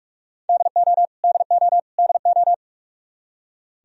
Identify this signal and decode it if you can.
Morse code, heard as a keyed tone, about 46 wpm, 700 Hz. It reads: DO DO DO
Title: Top 100 Words Repeated 45wpm